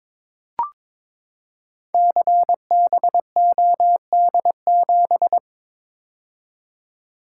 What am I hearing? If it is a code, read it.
Morse code, heard as a keyed tone, about 22 wpm, 700 Hz. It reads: CBOD7
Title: Code Groups - 5 Letter:Numbers - 22wpm